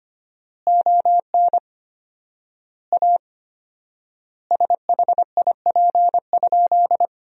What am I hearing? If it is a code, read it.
Morse code, heard as a keyed tone, about 25 wpm, 700 Hz. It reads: ON A SHIP?
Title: 4x Word-Spacing - Sentences from Top 400 Words - 25wpm